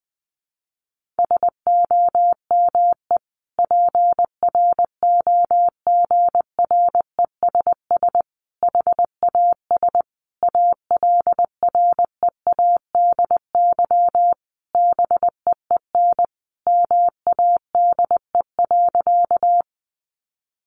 Morse code, heard as a keyed tone, about 20 wpm, 700 Hz. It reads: SOME PROGRESS HAS ALREADY BEEN MADE.
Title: Sentences from Top 1400 Words 20wpm